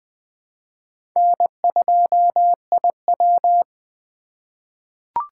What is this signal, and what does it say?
Morse code, heard as a keyed tone, about 20 wpm, 700 Hz. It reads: N2IW
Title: USA Call Signs - 4 Characters - 3x Character Spacing - 20wpm